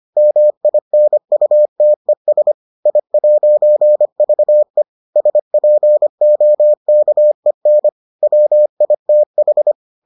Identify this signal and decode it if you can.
Morse code, heard as a keyed tone, about 25 wpm, 600 Hz. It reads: MINUTES I'VE SPOKEN WITH